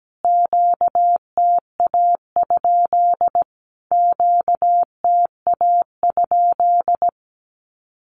Morse code, heard as a keyed tone, about 17 wpm, 700 Hz. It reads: QTA? QTA?
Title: Q-Codes Repeated 3x 17wpm